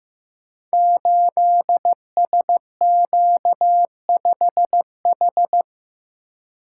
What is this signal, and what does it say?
Morse code, heard as a keyed tone, about 15 wpm, 700 Hz. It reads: 8SQ5H
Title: Code Groups - 5 Letter:Numbers - 15wpm